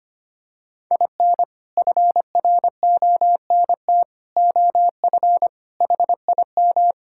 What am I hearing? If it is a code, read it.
Morse code, heard as a keyed tone, about 25 wpm, 700 Hz. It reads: IN FRONT OF HIM
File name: practice / 25wpm / Sentences with 4-Word Phrases - 25wpm